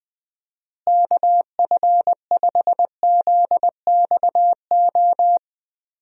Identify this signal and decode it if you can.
Morse code, heard as a keyed tone, about 20 wpm, 700 Hz. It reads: KF5ZXO
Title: USA Call Signs - 6 Characters - Repeated 3x - 20wpm